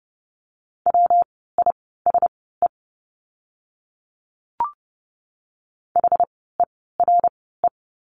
Morse code, heard as a keyed tone, about 30 wpm, spaced as for 17 wpm, 700 Hz. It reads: WISE HERE
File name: course / 30:10wpm / 115 Lesson 19f - Letter w - Words 30wpm